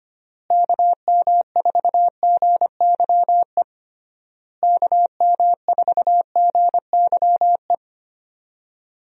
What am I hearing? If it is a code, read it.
Morse code, heard as a keyed tone, about 25 wpm, 700 Hz. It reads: KM4GYE KM4GYE
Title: USA Call Signs - 6 Characters - Repeated 3x - 25wpm